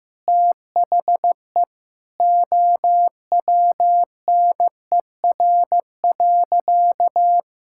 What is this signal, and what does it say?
Morse code, heard as a keyed tone, about 15 wpm, 700 Hz. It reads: THE OWNER.
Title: Sentences from Top 1600 Words 15wpm